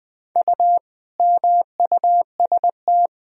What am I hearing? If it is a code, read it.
Morse code, heard as a keyed tone, about 21 wpm, 700 Hz. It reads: U MUST
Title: Sentences from Top 300 Words and Abbreviations 20wpm